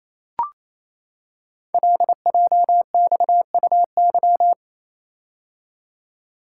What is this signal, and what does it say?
Morse code, heard as a keyed tone, about 29 wpm, 700 Hz. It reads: LJXUY
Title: Code Groups - 5 Letter:Numbers - 28wpm